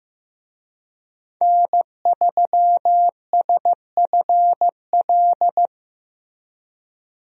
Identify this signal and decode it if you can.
Morse code, heard as a keyed tone, about 14 wpm, 700 Hz. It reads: N3SFL
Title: USA Call Signs - 5 Characters - Repeated 3x - 15wpm